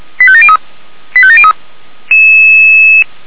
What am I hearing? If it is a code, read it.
Five-tone selective call (ZVEI), heard as quick runs of five tones, tones of about 70 ms. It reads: 85703 85703
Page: ZVEI